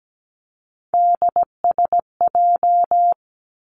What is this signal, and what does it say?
Morse code, heard as a keyed tone, about 17 wpm, 700 Hz. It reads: DSJ
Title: Three Challenging Letter-Number Combinations 17wpm